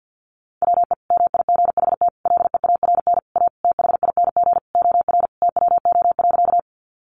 Morse code, heard as a keyed tone, about 50 wpm, 700 Hz. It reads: WE MIGHT LEARN A THING OR TWO.
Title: Sentences from Top 300 Words 50wpm